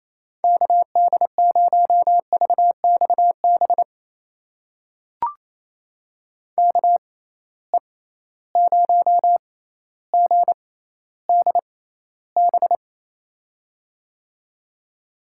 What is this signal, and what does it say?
Morse code, heard as a keyed tone, about 28 wpm, spaced as for 14 wpm, 700 Hz. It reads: KD0VXB KE0GDB
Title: USA Call Signs - 6 Characters - 6x Character Spacing - 28wpm